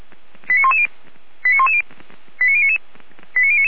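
Five-tone selective call (ZVEI), heard as quick runs of five tones, tones of about 70 ms. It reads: 89110 89110 89099 89099
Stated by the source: Five-tone ZVEI selective call received by SDR.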